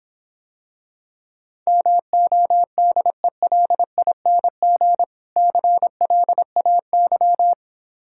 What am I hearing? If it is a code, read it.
Morse code, heard as a keyed tone, about 26 wpm, 700 Hz. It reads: MODELING CLAY